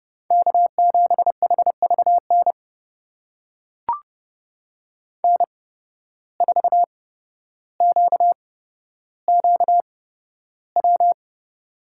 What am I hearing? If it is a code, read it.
Morse code, heard as a keyed tone, about 30 wpm, spaced as for 13 wpm, 700 Hz. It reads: K7HVN N4QQW
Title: USA Call Signs - 5 Characters - 8x Character Spacing - 30wpm